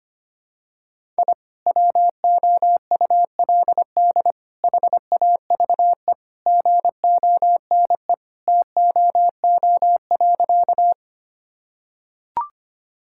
Morse code, heard as a keyed tone, about 25 wpm, 700 Hz. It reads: I WOULD HAVE GONE TOO.
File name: practice / 25wpm / Sentences with 2-Word Phrases - 25wpm